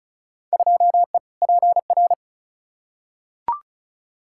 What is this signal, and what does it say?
Morse code, heard as a keyed tone, about 36 wpm, 700 Hz. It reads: 2E PR
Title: Field Day Exchanges Encore - 35wpm